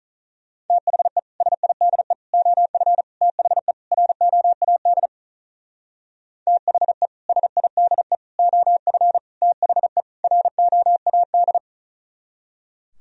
Morse code, heard as a keyed tone, about 45 wpm, 700 Hz. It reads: THE SIDE OF THE ROAD THE SIDE OF THE ROAD